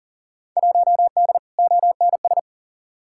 Morse code, heard as a keyed tone, about 37 wpm, 700 Hz. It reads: 1D ONS
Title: Field Day Exchanges - 40wpm